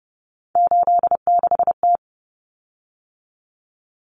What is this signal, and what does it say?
Morse code, heard as a keyed tone, about 30 wpm, 700 Hz. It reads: 86T
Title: Common Cut Numbers - 3-Digit 30wpm